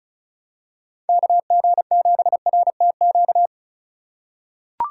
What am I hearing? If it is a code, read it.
Morse code, heard as a keyed tone, about 34 wpm, 700 Hz. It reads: KG7RTQ
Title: USA Call Signs - 6 Characters - 4x Character Spacing - 35wpm